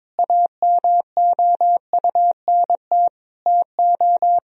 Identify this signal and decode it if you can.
Morse code, heard as a keyed tone, about 22 wpm, 700 Hz. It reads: AMOUNT TO
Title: Sentences from Top 1000 Words - Review Part 4 - 22wpm